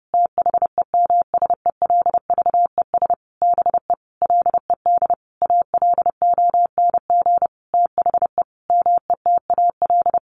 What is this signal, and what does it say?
Morse code, heard as a keyed tone, about 30 wpm, 700 Hz. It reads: THEMSELVES BE LED ALONG THE METAL
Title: A Sound Of Thunder 30wpm